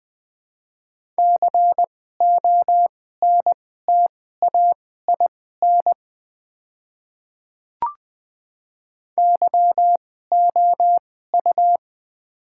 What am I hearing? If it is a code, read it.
Morse code, heard as a keyed tone, about 20 wpm, spaced as for 14 wpm, 700 Hz. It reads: CONTAIN YOU
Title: 2x Character Spacing - Top 500 Words - 20wpm